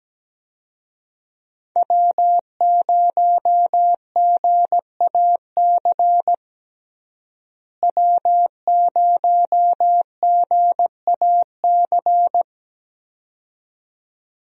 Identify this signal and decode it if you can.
Morse code, heard as a keyed tone, about 17 wpm, 700 Hz. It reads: W0GAC W0GAC
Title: USA Call Signs - 5 Characters - Repeated 2x - 17wpm